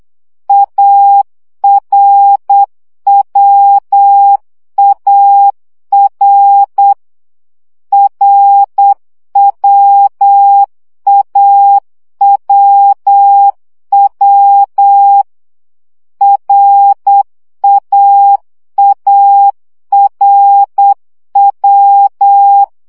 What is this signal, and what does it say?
Morse code, heard as a keyed tone, about 8 wpm, 800 Hz. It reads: ARWAR RWAWW RAARW
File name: CW gyakorlas a r w 7 baud